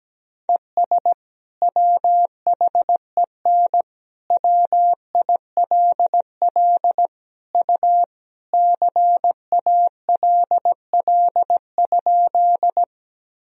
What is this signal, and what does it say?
Morse code, heard as a keyed tone, about 17 wpm, 700 Hz. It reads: ES WHEN WILL U CALL?
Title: Sentences from Top 100 Words and Abbreviations 17wpm